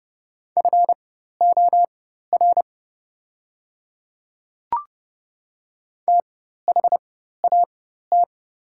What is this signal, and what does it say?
Morse code, heard as a keyed tone, about 32 wpm, spaced as for 14 wpm, 690 Hz. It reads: FOR THAT
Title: 4x Character Spacing - Top 100 Words - 30wpm